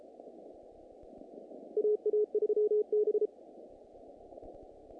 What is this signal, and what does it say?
Morse code, heard as a keyed tone, about 33 wpm, 420 Hz. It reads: AA3B
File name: contests / naqp-cw-k5tr-op-k5pi- -2100